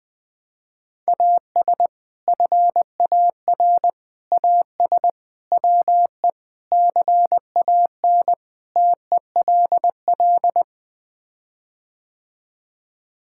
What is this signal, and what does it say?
Morse code, heard as a keyed tone, about 21 wpm, 700 Hz. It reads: AS FAR AS WE CAN TELL
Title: Sets of 6 Words - 20wpm